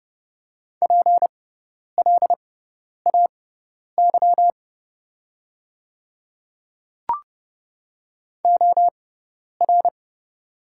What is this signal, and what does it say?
Morse code, heard as a keyed tone, about 30 wpm, spaced as for 10 wpm, 700 Hz. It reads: PLAY OR
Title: 6x Character Spacing - Top 200 Words - 30wpm